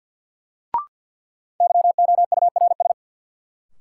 Morse code, heard as a keyed tone, about 53 wpm, 700 Hz. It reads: YOURS
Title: Sentences from Top 1500 Words - New Word Drill Alt - 50wpm